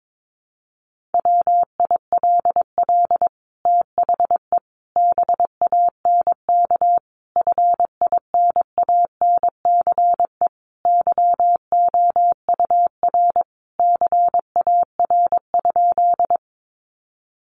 Morse code, heard as a keyed tone, about 22 wpm, 700 Hz. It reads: WILL THE BANK FINANCE YOUR CAR?
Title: Sentences from Top 1500 Words 22wpm